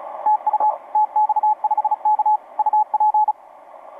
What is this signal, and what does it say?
Morse code, heard as a keyed tone, about 36 wpm, 840 Hz, with radio noise. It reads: TU TX5K UP
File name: tx5k sm6bzv etc 20m cw 04mar2013 0739